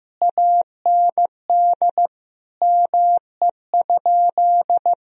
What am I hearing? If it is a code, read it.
Morse code, heard as a keyed tone, about 15 wpm, 700 Hz. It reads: AND ME?
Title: Sentences from Top 1000 Words - Review Part 2 - 15wpm